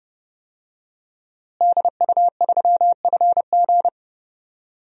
Morse code, heard as a keyed tone, about 32 wpm, 700 Hz. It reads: DU3FG
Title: DXCC-100 Call Signs - 5 Characters - Repeated 2x - 30wpm